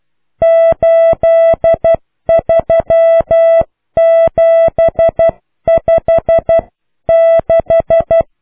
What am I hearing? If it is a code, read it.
Morse code, heard as a keyed tone, about 12 wpm, 660 Hz. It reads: 83756